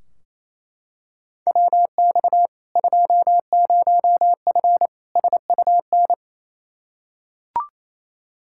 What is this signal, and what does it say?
Morse code, heard as a keyed tone, about 28 wpm, 700 Hz. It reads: WX 20F SUN
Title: Practice QSO - Exchange 2 - Info - 28wpm